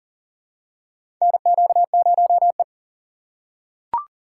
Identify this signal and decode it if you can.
Morse code, heard as a keyed tone, about 40 wpm, 700 Hz. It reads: NQ0E
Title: USA Call Signs - 4 Characters - 6x Character Spacing - 40wpm